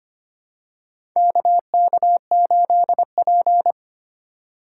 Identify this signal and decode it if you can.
Morse code, heard as a keyed tone, about 25 wpm, 700 Hz. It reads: KK8P
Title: USA Call Signs - 4 Characters - 2x Character Spacing - 25wpm